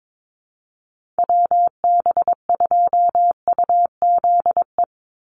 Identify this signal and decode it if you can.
Morse code, heard as a keyed tone, about 22 wpm, 700 Hz. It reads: WB2UZE
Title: Call Signs - 22wpm